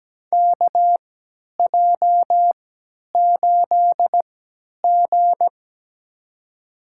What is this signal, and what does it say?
Morse code, heard as a keyed tone, about 17 wpm, spaced as for 10 wpm, 700 Hz. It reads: KJ8G
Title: USA Call Signs - 4 Characters - 3x Character Spacing - 17wpm